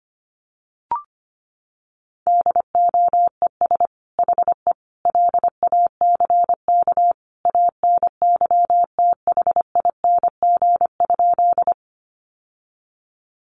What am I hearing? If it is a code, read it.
Morse code, heard as a keyed tone, about 25 wpm, 700 Hz. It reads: DOES HE LACK ANYTHING?
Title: Sentences from Top 1100 Words 25wpm